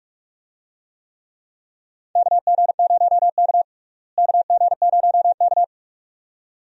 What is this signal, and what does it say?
Morse code, heard as a keyed tone, about 48 wpm, 700 Hz. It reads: KG0K KG0K
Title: USA Call Signs - 4 Characters - Repeated 2x - 45wpm